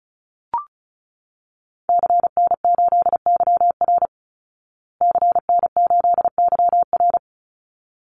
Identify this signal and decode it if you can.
Morse code, heard as a keyed tone, about 34 wpm, 700 Hz. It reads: CN8YR CN8YR
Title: Call Signs Repeated 3x - 35wpm